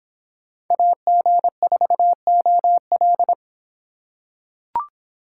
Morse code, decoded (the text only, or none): AG4OL